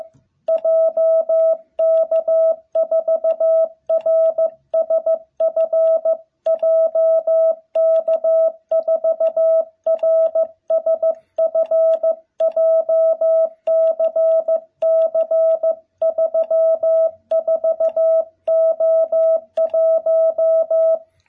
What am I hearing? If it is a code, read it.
Morse code, heard as a keyed tone, about 15 wpm, 660 Hz. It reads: JK4RSF JK4RSF JCC34O1
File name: jh8cbh / 2021cw